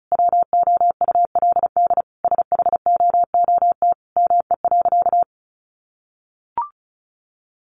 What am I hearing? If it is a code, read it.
Morse code, heard as a keyed tone, about 34 wpm, 700 Hz. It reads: WOULD SHOOT ME.
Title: Sentences from Top 1500 Words 35wpm